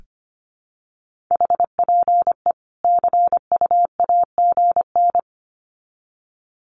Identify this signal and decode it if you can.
Morse code, heard as a keyed tone, about 25 wpm, 700 Hz. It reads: HPE CUAGN
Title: Common QSO Phrases 25wpm